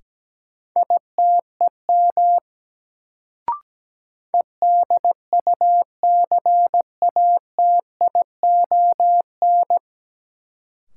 Morse code, heard as a keyed tone, about 17 wpm, 700 Hz. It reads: ITEM EDUCATION
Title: Sentences from Top 1000 Words - Review Part 6 - New Word Drill - 17wpm